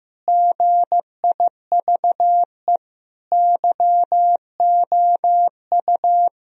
Morse code, heard as a keyed tone, about 15 wpm, 700 Hz. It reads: GIVE YOU
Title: Sentences from Top 1000 Words - Review Part 3 - 15wpm